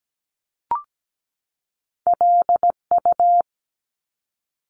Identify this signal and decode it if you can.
Morse code, heard as a keyed tone, about 17 wpm, 700 Hz. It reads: LU
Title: Top Two Letter Combinations in Words 17wpm